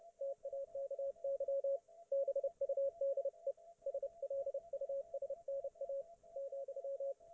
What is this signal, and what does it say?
Morse code, heard as a keyed tone, about 30 wpm, 550 Hz, with radio noise. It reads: TAKY BUDE SLUSNA ,